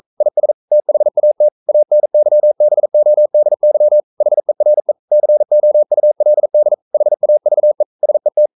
Morse code, heard as a keyed tone, about 42 wpm, 600 Hz. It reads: IS THAT ANYBODY HERE COULD HAVE SET
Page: Romanen Never af Ken Follet på engelsk.